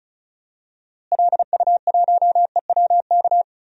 Morse code, heard as a keyed tone, about 36 wpm, 700 Hz. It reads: LU1EWK
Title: DXCC-100 Call Signs - 6 Characters - Repeated 3x - 35wpm